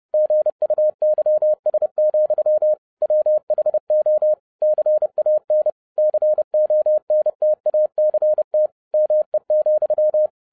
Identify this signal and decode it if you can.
Morse code, heard as a keyed tone, about 30 wpm, 610 Hz. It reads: GUYS, WHO CAN CONTACT ME,
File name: 2011-07-mp3 / n60 BACK FROM A BIG ACCIDENT 30wpm